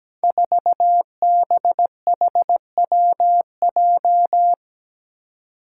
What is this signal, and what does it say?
Morse code, heard as a keyed tone, about 17 wpm, 700 Hz. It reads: 4BHWJ
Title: Five Challenging Letter-Number Combinations 17wpm